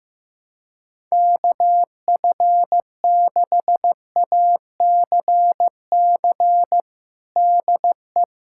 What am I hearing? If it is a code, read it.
Morse code, heard as a keyed tone, about 15 wpm, 700 Hz. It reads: KF6ACC DE
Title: USA Callsign DE Callsign 15wpm